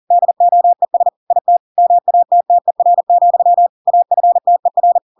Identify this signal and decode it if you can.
Morse code, heard as a keyed tone, about 40 wpm, 700 Hz. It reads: DOES IT MATTER, AFTER
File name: Pinocchio / Pinocchio ch8 40 40 ik1hge